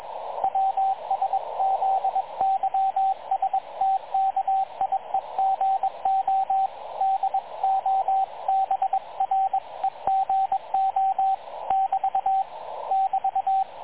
Morse code, heard as a keyed tone, about 21 wpm, 770 Hz, with radio noise. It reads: WSZYSTKIEGO DOBREGO <BT> <BT>
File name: dlazaawansowanych / nagraniazpasma / websdr recording 3567.0kHz